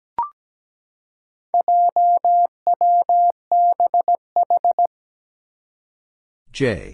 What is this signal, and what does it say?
Morse code, heard as a keyed tone, about 17 wpm, 700 Hz. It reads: JWBH
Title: Four Challenging Letter-Number Combinations 17wpm